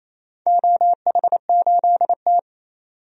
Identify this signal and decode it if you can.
Morse code, heard as a keyed tone, about 28 wpm, 700 Hz. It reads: OH8T
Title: DXCC-100 Call Signs - 4 Characters - 28wpm